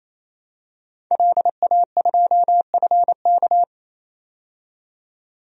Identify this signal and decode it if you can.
Morse code, heard as a keyed tone, about 28 wpm, 700 Hz. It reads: LA2FK
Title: DXCC-100 Call Signs - 5 Characters - 28wpm